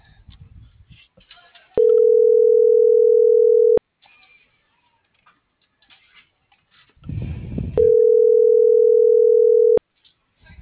Ringback tone (North America), 440 + 480 Hz.